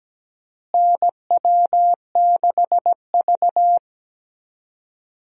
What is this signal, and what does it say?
Morse code, heard as a keyed tone, about 17 wpm, 700 Hz. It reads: NW6V
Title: USA Call Signs - 4 Characters - Repeated 3x - 17wpm